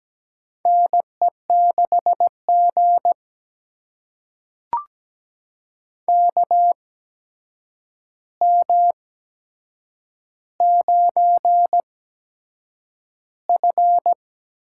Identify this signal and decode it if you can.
Morse code, heard as a keyed tone, about 17 wpm, spaced as for 7 wpm, 700 Hz. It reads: NE6G KM9F
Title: USA Call Signs - 4 Characters - 8x Character Spacing - 17wpm